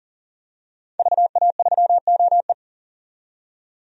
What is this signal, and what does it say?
Morse code, heard as a keyed tone, about 40 wpm, 700 Hz. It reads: VA3OE
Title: DXCC-100 Call Signs - 5 Characters - 40wpm